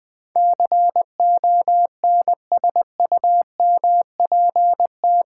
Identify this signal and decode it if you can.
Morse code, heard as a keyed tone, about 20 wpm, 700 Hz. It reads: CONSUMPT
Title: Build A Word - 4 Letters At A Time - 11-Letter Words - 20wpm